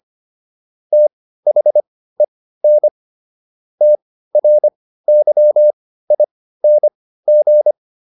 Morse code, heard as a keyed tone, about 25 wpm, spaced as for 15 wpm, 600 Hz. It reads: THEN TRYING